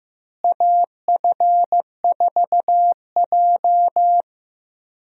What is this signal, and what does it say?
Morse code, heard as a keyed tone, about 15 wpm, 700 Hz. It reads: AF4J
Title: USA Call Signs - 4 Characters - Repeated 2x - 15wpm